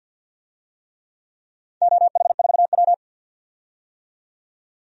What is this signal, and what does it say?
Morse code, heard as a keyed tone, about 52 wpm, 700 Hz. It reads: OH4W
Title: DXCC-100 Call Signs - 4 Characters - 50wpm